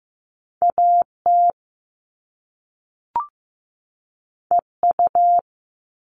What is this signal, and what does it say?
Morse code, heard as a keyed tone, about 15 wpm, 700 Hz. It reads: AT EU
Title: All Cut Numbers - 2-Digit 15wpm